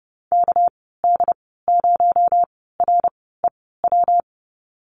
Morse code, heard as a keyed tone, about 31 wpm, spaced as for 17 wpm, 700 Hz. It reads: KD0REW
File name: Lesson 37g - Letter x - Call Signs 30wpm